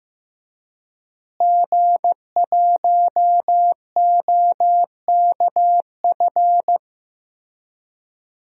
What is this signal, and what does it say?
Morse code, heard as a keyed tone, about 15 wpm, 700 Hz. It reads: G1OKF
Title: DXCC-100 Call Signs - 5 Characters - Repeated 2x - 15wpm